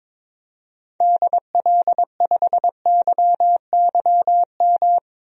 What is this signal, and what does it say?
Morse code, heard as a keyed tone, about 21 wpm, 700 Hz. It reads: DL5YYM
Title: DXCC-100 Call Signs - 6 Characters - 22wpm